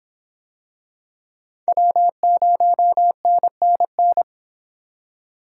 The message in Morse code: W0NNN

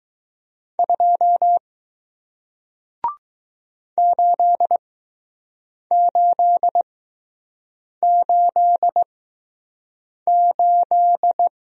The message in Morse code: 2 8888

